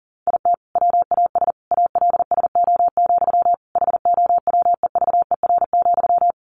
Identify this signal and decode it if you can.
Morse code, heard as a keyed tone, about 40 wpm, 700 Hz. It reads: IT WAS ALSO, HOWEVER,
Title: Sentences from Top 1400 Words 40wpm